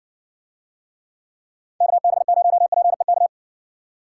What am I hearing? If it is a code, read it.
Morse code, heard as a keyed tone, about 58 wpm, 700 Hz. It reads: KB0PEK